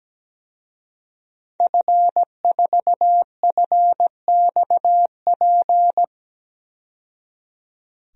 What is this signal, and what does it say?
Morse code, heard as a keyed tone, about 17 wpm, 700 Hz. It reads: F4FXP
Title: DXCC-100 Call Signs - 5 Characters - 17wpm